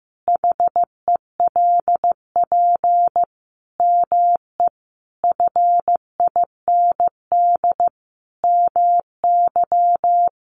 Morse code, heard as a keyed tone, about 15 wpm, 700 Hz. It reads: HELP ME FIND MY
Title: Sentences from Top 200 Words 15wpm